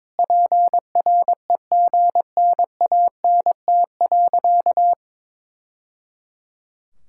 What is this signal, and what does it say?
Morse code, heard as a keyed tone, about 22 wpm, 700 Hz. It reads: PREGNANT.
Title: Sentences from Top 1000 Words - Review Part 6 - 22wpm